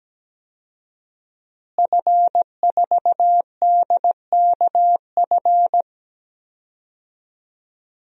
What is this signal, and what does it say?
Morse code, heard as a keyed tone, about 17 wpm, 700 Hz. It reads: F4DKF
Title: DXCC-100 Call Signs - 5 Characters - Repeated 2x - 17wpm